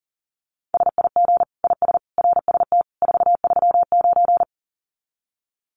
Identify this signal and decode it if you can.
Morse code, heard as a keyed tone, about 40 wpm, 700 Hz. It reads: SIG IS RST 439
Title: Common QSO Phrases 40wpm